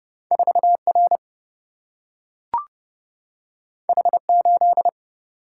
Morse code, heard as a keyed tone, about 31 wpm, 700 Hz. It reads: 4R H8
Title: Single-Letter Single-Number 30wpm